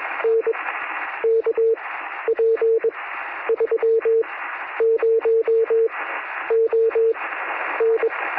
Morse code, heard as a keyed tone, about 21 wpm, spaced as for 10 wpm, 450 Hz, with noise. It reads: NK P30ON
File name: sm7lcb / 1957z oh5lk 23cm